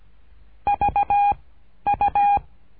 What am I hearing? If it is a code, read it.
Morse code, heard as a keyed tone, about 17 wpm, 810 Hz. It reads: V U
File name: cursos / CURSO DE TELEGRAFIA / Telegrafia - Trilha 13